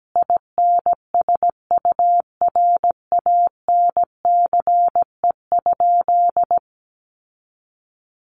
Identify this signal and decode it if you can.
Morse code, heard as a keyed tone, about 17 wpm, 700 Hz. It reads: INSURANCE?
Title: Sentences from Top 1100 Words 17wpm